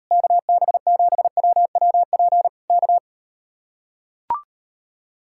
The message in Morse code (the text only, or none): KB7WWP K